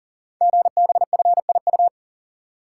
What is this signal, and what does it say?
Morse code, heard as a keyed tone, about 38 wpm, 690 Hz. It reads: GBFIU
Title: Code Groups - 5 Letters - 40wpm